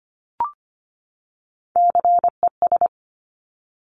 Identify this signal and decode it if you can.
Morse code, heard as a keyed tone, about 25 wpm, 700 Hz. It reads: CES
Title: Top Three Letter Combinations in Words 25wpm